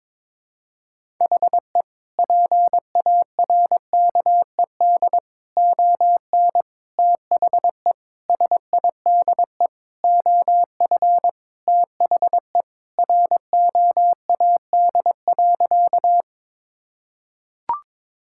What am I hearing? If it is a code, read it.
Morse code, heard as a keyed tone, about 22 wpm, 700 Hz. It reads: HE PARKED ON THE SIDE OF THE ROAD.
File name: practice / 22wpm / Sentences with 4-Word Phrases - 22wpm